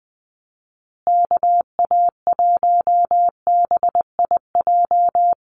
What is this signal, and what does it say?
Morse code, heard as a keyed tone, about 19 wpm, 700 Hz. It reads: KA1BIJ
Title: USA Call Signs Repeated 2x 20wpm